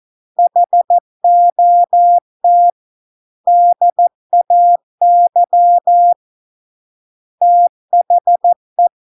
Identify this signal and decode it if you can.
Morse code, heard as a keyed tone, about 14 wpm, 700 Hz. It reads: HOT DAY THE